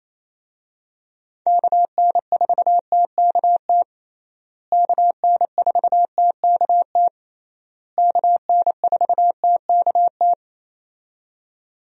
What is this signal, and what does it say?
Morse code, heard as a keyed tone, about 28 wpm, 700 Hz. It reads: KN4TKT KN4TKT KN4TKT
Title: USA Call Signs - 6 Characters - Repeated 3x - 28wpm